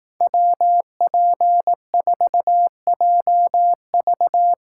Morse code, heard as a keyed tone, about 18 wpm, 700 Hz. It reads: WP4JV
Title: Speed Racing - USA Call Signs 15wpm